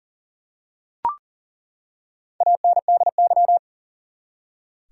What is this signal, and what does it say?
Morse code, heard as a keyed tone, about 40 wpm, 700 Hz. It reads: ANDY
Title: USA Common Names - Max 4 Letters - 40wpm